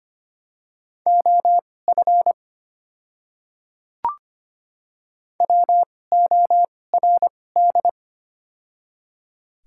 Morse code, heard as a keyed tone, about 25 wpm, spaced as for 18 wpm, 700 Hz. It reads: OF WORD
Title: 2x Character Spacing - Top 100 Words - 25wpm